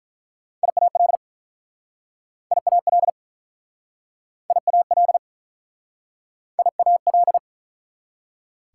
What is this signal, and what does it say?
Morse code, heard as a keyed tone, about 49 wpm, 700 Hz. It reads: IAL IAL IAL IAL